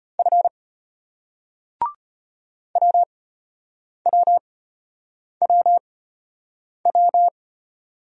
Morse code, heard as a keyed tone, about 33 wpm, spaced as for 8 wpm, 700 Hz. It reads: F WWWW